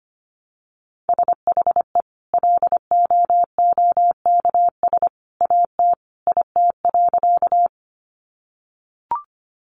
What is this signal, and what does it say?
Morse code, heard as a keyed tone, about 25 wpm, 700 Hz. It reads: SHE LOOKS AT IT.
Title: Additional Easy Phrases 25wpm